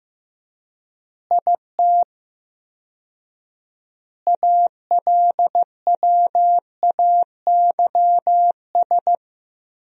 Morse code, heard as a keyed tone, about 14 wpm, 700 Hz. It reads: IT ALWAYS
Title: 4x Word-Spacing - Sentences from Top 300 Words - 15wpm